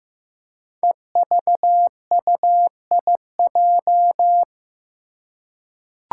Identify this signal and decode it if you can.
Morse code, heard as a keyed tone, about 15 wpm, 700 Hz. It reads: EVUIJ
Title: Five Challenging Letter-Number Combinations 15wpm